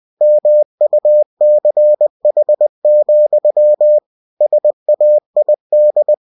Morse code, heard as a keyed tone, about 20 wpm, 600 Hz. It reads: MUCH, SAID
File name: cw-books / winnie-the-pooh / winnie-20wpm-page-0017